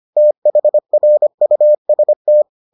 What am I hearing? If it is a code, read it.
Morse code, heard as a keyed tone, about 25 wpm, 600 Hz. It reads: THRUST